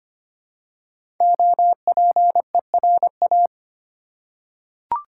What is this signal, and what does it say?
Morse code, heard as a keyed tone, about 25 wpm, 700 Hz. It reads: OPERA
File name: 5 Letter Words - First 3 Letters Given - 25wpm